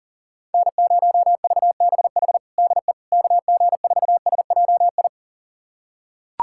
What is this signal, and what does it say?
Morse code, heard as a keyed tone, about 37 wpm, 700 Hz. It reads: N0VBH DE KG4SJI
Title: USA Callsign DE Callsign 40wpm